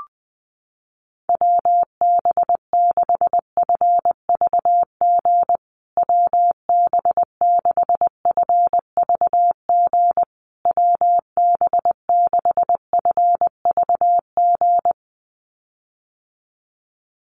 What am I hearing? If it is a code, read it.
Morse code, heard as a keyed tone, about 20 wpm, 700 Hz. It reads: WB6FVG WB6FVG WB6FVG
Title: USA Call Signs Repeated 3x 20wpm